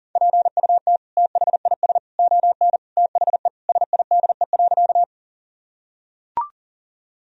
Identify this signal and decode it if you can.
Morse code, heard as a keyed tone, about 40 wpm, 700 Hz. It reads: PUT THIS ON THE SIDE.
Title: Sentences from Top 1000 Words - Review Part 1 - 40wpm